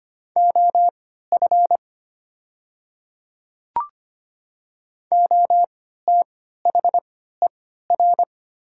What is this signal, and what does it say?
Morse code, heard as a keyed tone, about 25 wpm, spaced as for 14 wpm, 700 Hz. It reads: OF OTHER
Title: 3x Character Spacing - Top 100 Words - 25wpm